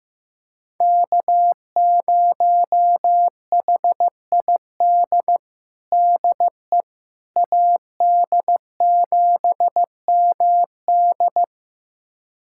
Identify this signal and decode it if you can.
Morse code, heard as a keyed tone, about 15 wpm, 700 Hz. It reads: K0HID DE AD7MD
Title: USA Callsign DE Callsign - 5 Characters - 15wpm